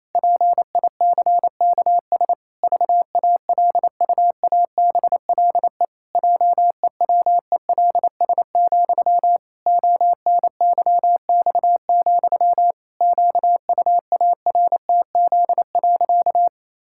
Morse code, heard as a keyed tone, about 28 wpm, 700 Hz. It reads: PICKS VALUABLE JEWELS, ONYX, QUARTZ.
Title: Pangram Challenge v2 - 28wpm